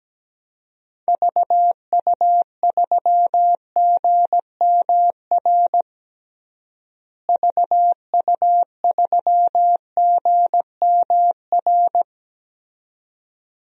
Morse code, heard as a keyed tone, about 17 wpm, 700 Hz. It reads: VU3GMR VU3GMR
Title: DXCC-100 Call Signs - 6 Characters - Repeated 2x - 17wpm